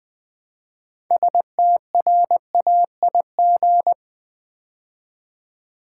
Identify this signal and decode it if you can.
Morse code, heard as a keyed tone, about 20 wpm, 690 Hz. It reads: STRAIG
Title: Build A Word - 3 Letters At A Time - 8-Letter Words - 20wpm